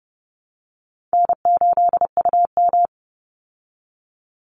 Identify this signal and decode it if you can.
Morse code, heard as a keyed tone, about 30 wpm, 700 Hz. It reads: N8UM
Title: Easy Call Signs Repeated 3x - 30wpm